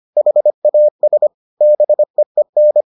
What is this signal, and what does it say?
Morse code, heard as a keyed tone, about 25 wpm, 600 Hz. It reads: HAS BEEN